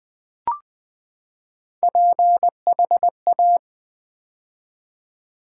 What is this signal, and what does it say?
Morse code, heard as a keyed tone, about 20 wpm, 700 Hz. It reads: PHA
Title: Build A Word - 3 Letters At A Time - 8-Letter Words - 20wpm